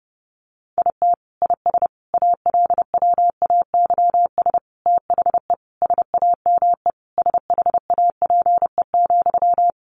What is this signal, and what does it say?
Morse code, heard as a keyed tone, about 30 wpm, 700 Hz. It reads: IT IS ALWAYS THE SAME SHAPE,
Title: Sentences from Top 1400 Words 30wpm